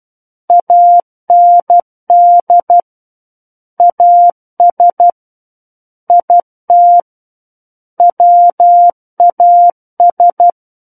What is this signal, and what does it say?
Morse code, heard as a keyed tone, about 12 wpm, 700 Hz. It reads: AND AS IT WAS
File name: Book One / 03-Chapter 3 - 12 WPM